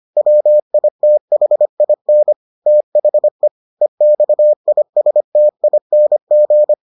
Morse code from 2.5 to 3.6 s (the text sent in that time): THE